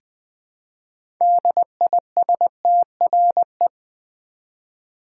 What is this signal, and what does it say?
Morse code, heard as a keyed tone, about 20 wpm, 700 Hz. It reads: DISTRE
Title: Build A Word - 2 Letters At A Time - 8-Letter Words - 20wpm